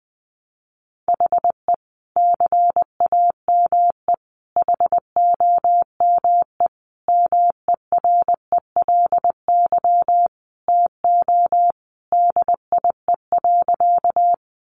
Morse code, heard as a keyed tone, about 19 wpm, 700 Hz. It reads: HE CAME HOME MERELY TO DIE.
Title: Sentences from Top 1400 Words 20wpm